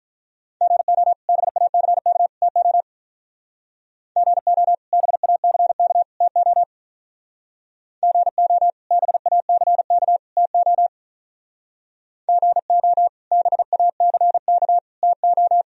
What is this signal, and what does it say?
Morse code, heard as a keyed tone, about 43 wpm, 700 Hz. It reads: GO BACK TO GO BACK TO GO BACK TO GO BACK TO
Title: Speed-Racing - Sets of 3 Words 35wpm